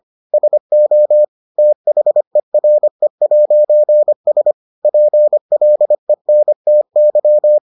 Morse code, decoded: SO THERE'S PLENTY

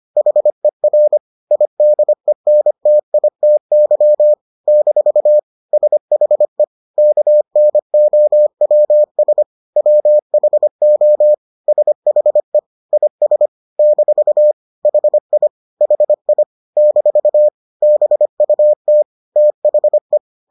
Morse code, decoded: HER IDENTITY - SHE KNOWS WHO SHE IS - HI HI - BUT THE

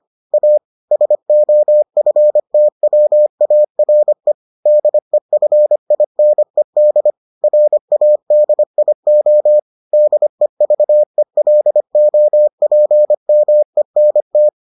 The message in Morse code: A SOFTWARE DEFINED RADIO DEVELOPMENT

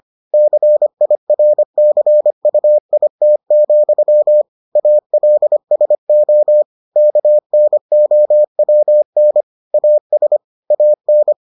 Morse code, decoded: CIRCUIT, ALSO KNOWN AS AN

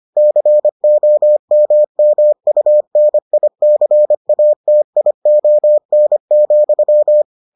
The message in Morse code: COMMUNICATION,

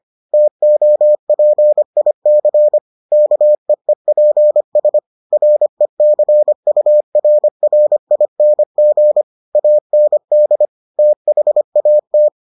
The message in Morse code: TOPIC KEEPS RECURRING AND THAT

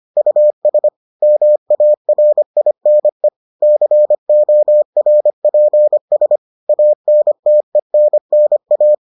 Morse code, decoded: US MARINE CORPS ANTENNA

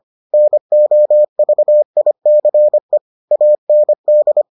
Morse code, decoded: NOVICE AND

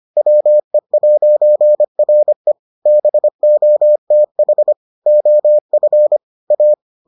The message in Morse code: WE'RE BOTH OF A

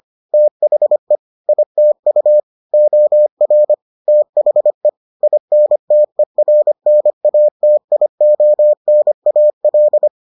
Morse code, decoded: THE ITU OR THE INTERNATIONAL

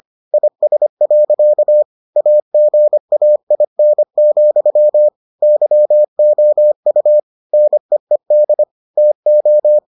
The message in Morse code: IS. AGAIN, YOU NEED TO

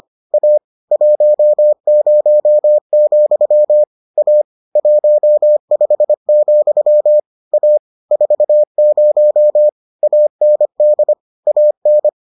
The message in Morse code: A 10, A 15, A 40 AND AN

